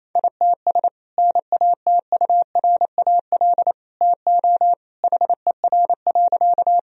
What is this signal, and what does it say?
Morse code, heard as a keyed tone, about 28 wpm, 700 Hz. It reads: ITS NATURAL TO HER.
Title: Sentences from Top 1000 Words - Review Part 4 - 28wpm